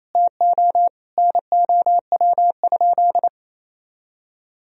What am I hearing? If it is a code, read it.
Morse code, heard as a keyed tone, about 28 wpm, 700 Hz. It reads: TO NOW?
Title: Sentences from Top 100 Words and Abbreviations 28wpm